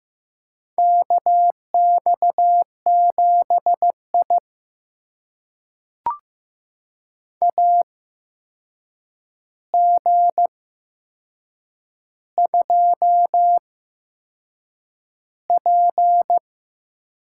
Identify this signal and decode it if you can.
Morse code, heard as a keyed tone, about 15 wpm, spaced as for 6 wpm, 700 Hz. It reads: KX7I AG2P
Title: USA Call Signs - 4 Characters - 8x Character Spacing - 15wpm